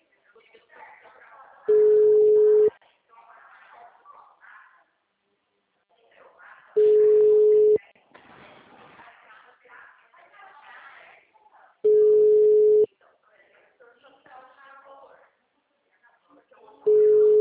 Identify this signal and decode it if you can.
Ringback tone (Europe, 425 Hz), with speech.
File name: out- -2004